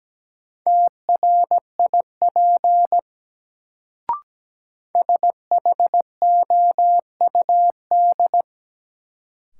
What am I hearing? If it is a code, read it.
Morse code, heard as a keyed tone, about 17 wpm, 700 Hz. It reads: TRIP SHOUD
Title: Sentences from Top 1000 Words - Review Part 4 - New Word Drill - 17wpm